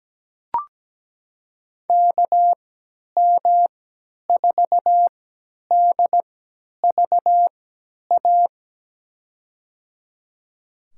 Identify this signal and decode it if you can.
Morse code, heard as a keyed tone, about 17 wpm, spaced as for 10 wpm, 700 Hz. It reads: KM4DVA
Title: USA Call Signs - 6 Characters - 3x Character Spacing - 17wpm